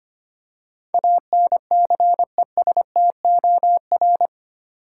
Morse code, heard as a keyed tone, about 25 wpm, 700 Hz. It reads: ANCESTOR
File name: Eight Letter Words - 25wpm